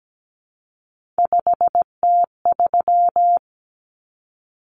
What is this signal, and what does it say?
Morse code, heard as a keyed tone, about 17 wpm, 700 Hz. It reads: 5T3
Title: Common Cut Numbers - 3-Digit 17wpm